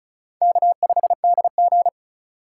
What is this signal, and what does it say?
Morse code, heard as a keyed tone, about 34 wpm, 700 Hz. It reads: K5DG
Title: USA Call Signs - 4 Characters - 2x Character Spacing - 35wpm